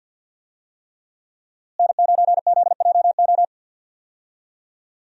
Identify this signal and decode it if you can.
Morse code, heard as a keyed tone, about 51 wpm, 700 Hz. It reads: N9ZJO
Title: USA Call Signs - 5 Characters - Repeated 3x - 50wpm